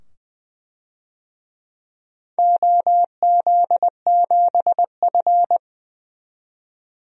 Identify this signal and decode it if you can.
Morse code, heard as a keyed tone, about 20 wpm, 700 Hz. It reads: OZ7F
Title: DXCC-100 Call Signs - 4 Characters - 20wpm